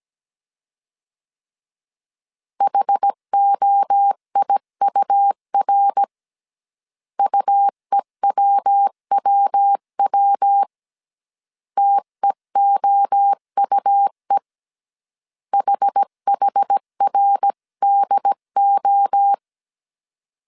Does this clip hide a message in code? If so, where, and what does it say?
Yes: Morse code from 2.6 to 19.3 s, heard as a keyed tone, about 17 wpm, 800 Hz. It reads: HOIUR UEWWW TEOUE HHRDO